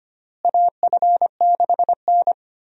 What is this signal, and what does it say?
Morse code, heard as a keyed tone, about 25 wpm, 700 Hz. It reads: AF6N
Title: USA Call Signs - 4 Characters - Repeated 2x - 25wpm